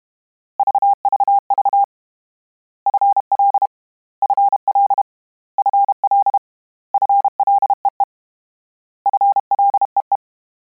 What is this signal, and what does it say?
Morse code, heard as a keyed tone, about 32 wpm, 800 Hz. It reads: VVV FL FL FL FLEE FLEE
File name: Combinations / Combinations-mp3s / Combinations@30